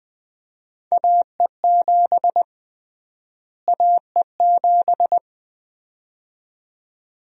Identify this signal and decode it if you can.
Morse code, heard as a keyed tone, about 20 wpm, 700 Hz. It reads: AE7 AE7
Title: USA Call Sign Prefixes - Repeated 2x - 20wpm